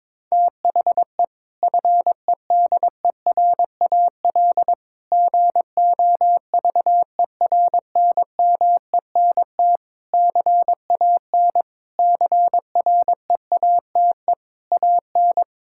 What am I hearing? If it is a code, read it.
Morse code, heard as a keyed tone, about 22 wpm, 700 Hz. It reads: THE FEDERAL GOVERNMENT CAN CREATE AN
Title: Sentences from Top 1600 Words 22wpm